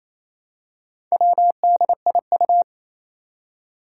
Morse code, heard as a keyed tone, about 29 wpm, 700 Hz. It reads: WDIU
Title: Four Challenging Letter-Number Combinations 28wpm